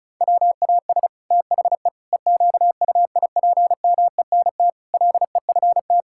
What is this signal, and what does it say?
Morse code, heard as a keyed tone, about 34 wpm, 700 Hz. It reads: WAS THE EQUIPMENT LEFT
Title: Sentences from Top 1000 Words - Review Part 5 - 35wpm